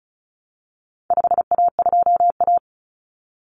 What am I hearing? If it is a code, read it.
Morse code, heard as a keyed tone, about 34 wpm, 700 Hz. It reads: 5A2A
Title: Common Cut Numbers - 4-Digit 35wpm